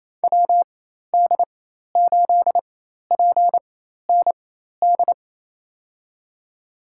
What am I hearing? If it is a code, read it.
Morse code, heard as a keyed tone, about 28 wpm, spaced as for 13 wpm, 700 Hz. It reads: WD8PND
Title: USA Call Signs - 4x Character Spacing - 28wpm